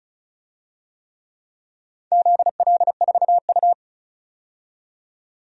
Morse code, heard as a keyed tone, about 36 wpm, 700 Hz. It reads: ZL4U
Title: DXCC-100 Call Signs - 4 Characters - 35wpm